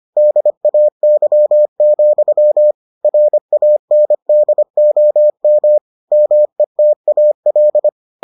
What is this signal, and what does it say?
Morse code, heard as a keyed tone, about 25 wpm, 600 Hz. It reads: DAY, RANDOM METAL